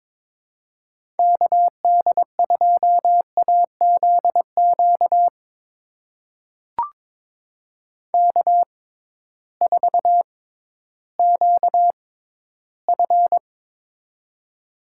Morse code, heard as a keyed tone, about 22 wpm, spaced as for 13 wpm, 700 Hz. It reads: KD2AZQ K4QF
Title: USA Call Signs - 6x Character Spacing - 22wpm